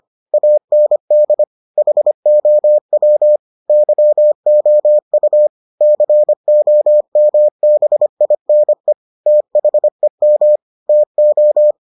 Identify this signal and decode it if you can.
Morse code, heard as a keyed tone, about 25 wpm, 600 Hz. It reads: AND HOW YOU COMBINE THEM TO